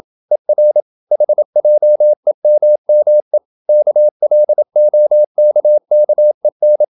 Morse code, heard as a keyed tone, about 27 wpm, 600 Hz. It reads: ER HJEMME KLOKKEN
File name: Ferie QRQ 27wpm